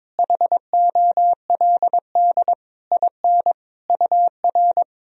Morse code, heard as a keyed tone, about 22 wpm, 700 Hz. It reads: HOLD IN UR
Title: Sentences from Top 400 Words and Abbreviations 22wpm